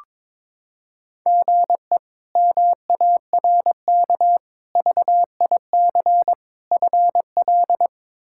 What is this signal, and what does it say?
Morse code, heard as a keyed tone, about 21 wpm, 700 Hz. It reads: GE MARK VIC FL
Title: K1USN SST QSO Practice - 22wpm